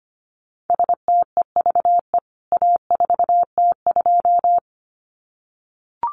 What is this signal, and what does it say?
Morse code, heard as a keyed tone, about 25 wpm, 700 Hz. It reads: STEVE A4T2
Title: CWT Exchange 25wpm